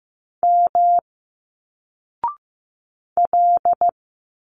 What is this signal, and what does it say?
Morse code, heard as a keyed tone, about 15 wpm, 700 Hz. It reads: M L
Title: Single Letters 15wpm